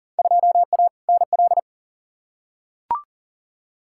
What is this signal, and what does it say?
Morse code, heard as a keyed tone, about 40 wpm, 700 Hz. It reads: 2A NL
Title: Field Day Exchanges Encore - 40wpm